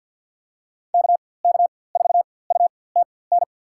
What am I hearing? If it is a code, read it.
Morse code, heard as a keyed tone, about 52 wpm, spaced as for 24 wpm, 690 Hz. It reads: KK4UTN
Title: USA Call Signs - 4x Character Spacing - 50wpm